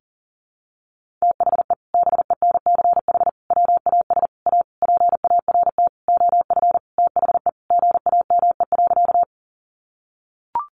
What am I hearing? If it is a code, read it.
Morse code, heard as a keyed tone, about 40 wpm, 700 Hz. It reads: THE BENCH WAS A PART OF THE GAME.
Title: Sentences from Top 1400 Words 40wpm